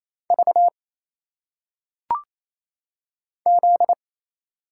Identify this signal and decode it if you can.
Morse code, heard as a keyed tone, about 27 wpm, 700 Hz. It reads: V Z
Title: Single Letters - Phonetic Pronunciation 28wpm